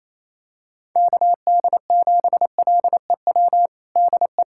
Morse code, heard as a keyed tone, about 28 wpm, 700 Hz. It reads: KD7LEW DE
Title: USA Callsign DE Callsign 28wpm